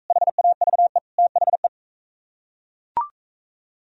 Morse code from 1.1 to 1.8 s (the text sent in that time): THE